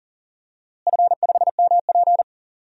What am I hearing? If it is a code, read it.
Morse code, heard as a keyed tone, about 40 wpm, 700 Hz. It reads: F5MP
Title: DXCC-100 Call Signs - 4 Characters - Repeated 2x - 40wpm